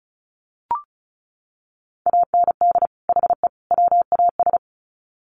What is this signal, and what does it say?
Morse code, heard as a keyed tone, about 34 wpm, 700 Hz. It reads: AND HE WAS
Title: Sets of 3 Words 35wpm